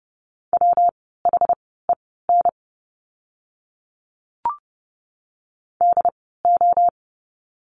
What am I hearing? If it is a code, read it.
Morse code, heard as a keyed tone, about 30 wpm, spaced as for 17 wpm, 700 Hz. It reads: WHEN DO
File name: course / 30:10wpm / 115 Lesson 19f - Letter w - Words 30wpm